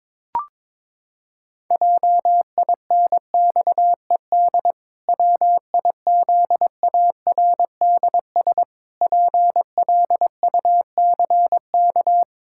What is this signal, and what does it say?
Morse code, heard as a keyed tone, about 21 wpm, 700 Hz. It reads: JINXED WIZARDS PLUCK
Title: Pangram Challenge v2 - 22wpm